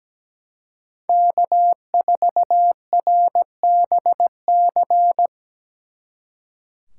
Morse code, heard as a keyed tone, about 19 wpm, 700 Hz. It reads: K4RBC
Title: Speed Racing - USA Call Signs 17wpm